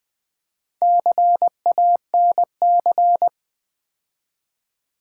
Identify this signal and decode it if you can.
Morse code, heard as a keyed tone, about 20 wpm, 700 Hz. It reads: CANC
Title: Build A Word - 2 Letters At A Time - 6-Letter Words - 20wpm